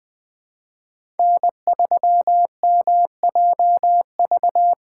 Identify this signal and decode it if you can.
Morse code, heard as a keyed tone, about 20 wpm, 700 Hz. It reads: N3MJV
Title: USA Call Signs - 2x Character Spacing - 20wpm